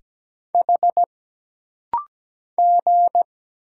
Morse code, heard as a keyed tone, about 17 wpm, 700 Hz. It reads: H G
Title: Single Letters - Rapid-Fire 17wpm